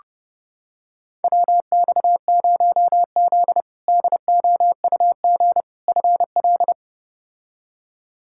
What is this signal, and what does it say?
Morse code, heard as a keyed tone, about 30 wpm, 700 Hz. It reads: WX0Z DOUG FL
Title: K1USN SST QSO Practice - 30wpm